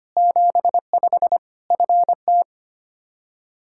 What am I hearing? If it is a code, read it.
Morse code, heard as a keyed tone, about 25 wpm, 700 Hz. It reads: 75 FT
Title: K1USN SST QSO Practice 25wpm